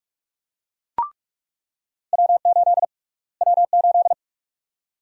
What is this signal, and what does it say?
Morse code, heard as a keyed tone, about 43 wpm, 700 Hz. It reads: W8 W8
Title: USA Call Sign Prefixes - Repeated 2x - 45wpm